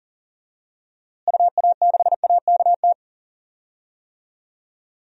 Morse code, heard as a keyed tone, about 40 wpm, 700 Hz. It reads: UA6AKT
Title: DXCC-100 Call Signs - 6 Characters - Repeated 3x - 40wpm